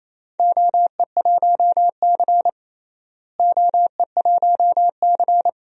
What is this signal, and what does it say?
Morse code, heard as a keyed tone, about 28 wpm, 700 Hz. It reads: OE1C OE1C
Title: DXCC-100 Call Signs - 4 Characters - Repeated 2x - 28wpm